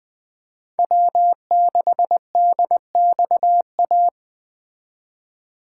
Morse code, heard as a keyed tone, about 20 wpm, 700 Hz. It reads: W6DXA
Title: Speed Racing - USA Call Signs 17wpm